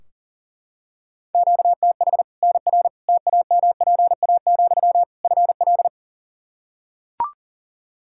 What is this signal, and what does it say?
Morse code, heard as a keyed tone, about 40 wpm, 700 Hz. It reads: QTH NR TAMPA, FL
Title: Practice QSO - Exchange 1 - Beginning - 40wpm